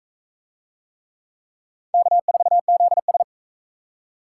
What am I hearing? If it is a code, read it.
Morse code, heard as a keyed tone, about 39 wpm, 700 Hz. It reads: K4ZS